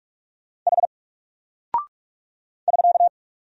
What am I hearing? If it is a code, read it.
Morse code, heard as a keyed tone, about 44 wpm, 700 Hz. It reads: H <SK>
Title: Single Character (Letter:Number:Punctuation:Prosign) Alt 45wpm